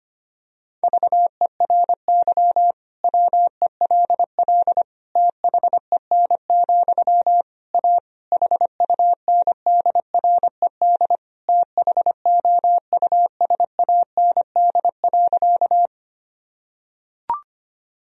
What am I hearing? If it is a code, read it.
Morse code, heard as a keyed tone, about 25 wpm, 700 Hz. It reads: VERY WELL THEN, A HUNDRED THOUSAND.
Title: Sentences from Top 1000 Words - Review Part 3 - 25wpm